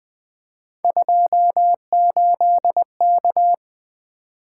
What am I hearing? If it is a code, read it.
Morse code, heard as a keyed tone, about 20 wpm, 700 Hz. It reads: 28K
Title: Three Letter-Number Combinations - 20wpm